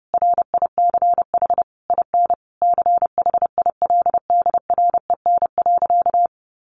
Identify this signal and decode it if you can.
Morse code, heard as a keyed tone, about 30 wpm, 700 Hz. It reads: RICH IN CHILDREN.
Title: Sentences from Top 1100 Words 30wpm